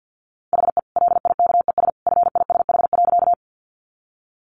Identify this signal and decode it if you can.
Morse code, heard as a keyed tone, about 48 wpm, 700 Hz. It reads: HE LIKES FISH.
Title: Additional Easy Phrases 50wpm